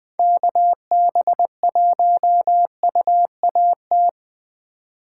Morse code, heard as a keyed tone, about 21 wpm, 700 Hz. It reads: KB1UAT